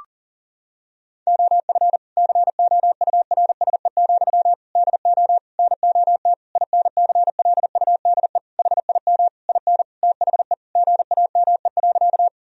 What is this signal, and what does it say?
Morse code, heard as a keyed tone, about 37 wpm, 700 Hz. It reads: OF COURSE, DO NOT INCLUDE HIM IN THE GAME.
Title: Sentences from Top 1000 Words - Review Part 3 - 40wpm